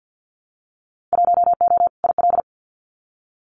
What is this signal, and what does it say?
Morse code, heard as a keyed tone, about 52 wpm, 700 Hz. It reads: 1O IL
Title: Winter Field Day Exchanges 50wpm